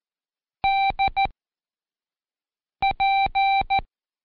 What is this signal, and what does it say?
Morse code, heard as a keyed tone, about 14 wpm, 780 Hz. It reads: D P
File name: Seg29-Day4-CheckTest 4